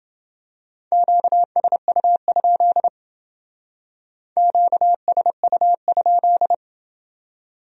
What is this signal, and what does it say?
Morse code, heard as a keyed tone, about 29 wpm, 700 Hz. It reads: QSU? QSU?